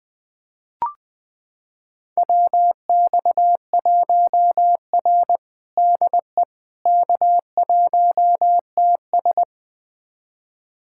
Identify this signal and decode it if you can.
Morse code, heard as a keyed tone, about 20 wpm, 700 Hz. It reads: WX1R DE K1TS
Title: USA Callsign DE Callsign - 4 Characters - 20wpm